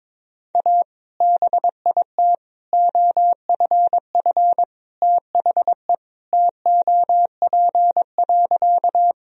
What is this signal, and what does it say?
Morse code, heard as a keyed tone, about 21 wpm, 700 Hz. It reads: A BIT OFF THE TOP.
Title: Sentences from Top 1000 Words - Review Part 2 - 22wpm